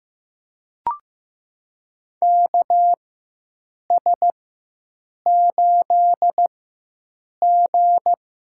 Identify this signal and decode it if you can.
Morse code, heard as a keyed tone, about 15 wpm, spaced as for 7 wpm, 700 Hz. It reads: KS8G
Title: USA Call Signs - 4x Character Spacing - 15wpm